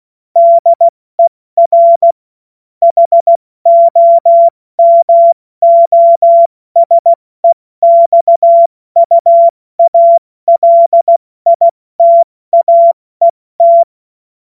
Morse code, heard as a keyed tone, about 16 wpm, 680 Hz. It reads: DER HOMOSEXUALITAET
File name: Morsekurs / Witze / Witz 19. April 2022 70 F80